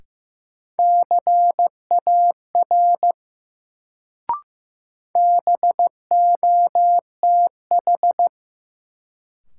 Morse code, heard as a keyed tone, about 15 wpm, 700 Hz. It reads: CAR BOTH
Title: Sentences from Top 300 Words - New Word Drill - 15wpm